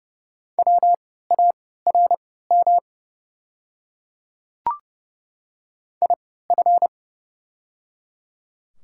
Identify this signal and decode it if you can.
Morse code, heard as a keyed tone, about 30 wpm, spaced as for 17 wpm, 700 Hz. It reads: WARM IF
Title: 3x Character Spacing - Top 500 Words - 30wpm